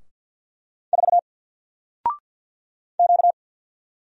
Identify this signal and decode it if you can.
Morse code, heard as a keyed tone, about 50 wpm, 700 Hz. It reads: 4 Q
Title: Single Character (Letter:Number:Punctuation:Prosign) 50wpm